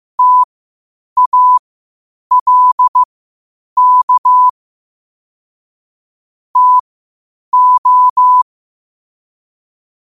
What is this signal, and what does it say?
Morse code, heard as a keyed tone, about 15 wpm, spaced as for 8 wpm, 1000 Hz. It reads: TALK TO
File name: qotdfiles / 8wpm